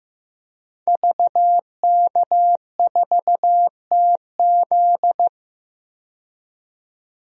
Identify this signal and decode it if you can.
Morse code, heard as a keyed tone, about 15 wpm, 700 Hz. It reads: VK4TZ
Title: DXCC-100 Call Signs - 5 Characters - 15wpm